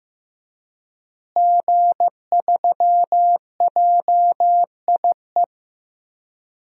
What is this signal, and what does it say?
Morse code, heard as a keyed tone, about 15 wpm, 700 Hz. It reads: G3JIE
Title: DXCC-100 Call Signs - 5 Characters - Repeated 3x - 15wpm